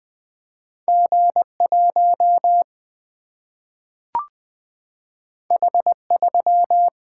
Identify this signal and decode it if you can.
Morse code, heard as a keyed tone, about 20 wpm, 700 Hz. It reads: G1 H3
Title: Single-Letter Single-Number 20wpm